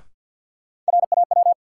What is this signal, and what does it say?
Morse code, heard as a keyed tone, about 53 wpm, 700 Hz. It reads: RAW
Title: Sentences from Top 1500 Words - New Word Drill Alt - 50wpm